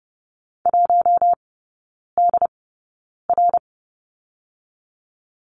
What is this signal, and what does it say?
Morse code, heard as a keyed tone, about 30 wpm, spaced as for 9 wpm, 700 Hz. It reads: 1DR
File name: course / 30:10wpm / 087 Lesson 15e - Number 5 - Three Characters 30wpm